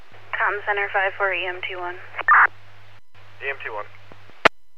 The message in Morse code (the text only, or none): none